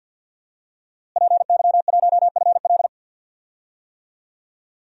Morse code, heard as a keyed tone, about 47 wpm, 700 Hz. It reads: PY1FL
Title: DXCC-100 Call Signs - 5 Characters - Repeated 2x - 50wpm